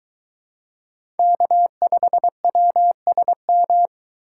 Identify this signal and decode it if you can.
Morse code, heard as a keyed tone, about 22 wpm, 700 Hz. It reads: K5WSM